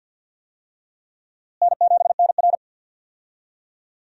Morse code, heard as a keyed tone, about 52 wpm, 700 Hz. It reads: N7NR
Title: USA Call Signs - 4 Characters - Repeated 2x - 50wpm